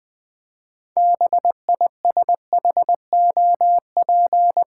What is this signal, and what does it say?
Morse code, heard as a keyed tone, about 20 wpm, 700 Hz. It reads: BISHOP
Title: 6 Letter Words - First 3 Letters Given - 20wpm